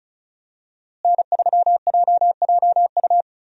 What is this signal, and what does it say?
Morse code, heard as a keyed tone, about 36 wpm, 700 Hz. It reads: N3JJU
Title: USA Call Signs - 5 Characters - Repeated 2x - 35wpm